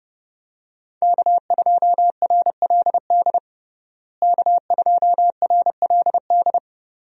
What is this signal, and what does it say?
Morse code, heard as a keyed tone, about 30 wpm, 700 Hz. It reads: K2RLD K2RLD
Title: USA Call Signs - 5 Characters - Repeated 2x - 30wpm